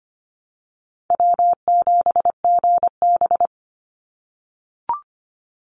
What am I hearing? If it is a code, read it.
Morse code, heard as a keyed tone, about 25 wpm, 700 Hz. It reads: W7GB
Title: Easy Call Signs - 25wpm